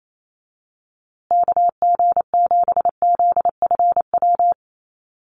Morse code, heard as a keyed tone, about 28 wpm, 700 Hz. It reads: KG7ZFW
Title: USA Call Signs Repeated 2x 28wpm